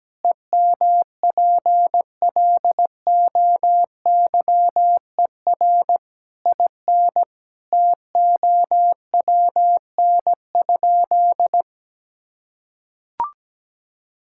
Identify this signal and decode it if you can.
Morse code, heard as a keyed tone, about 17 wpm, 700 Hz. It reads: EMPLOYER IN TOWN?
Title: Sentences from Top 1600 Words 17wpm